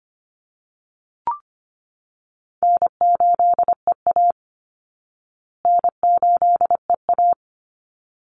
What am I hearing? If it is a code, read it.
Morse code, heard as a keyed tone, about 25 wpm, 700 Hz. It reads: N8EA N8EA
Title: Easy Call Signs Repeated 2x - 25wpm